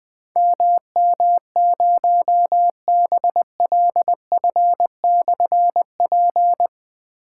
Morse code, heard as a keyed tone, about 20 wpm, 700 Hz. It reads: MM0BLF/P
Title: Difficult Call Signs 20wpm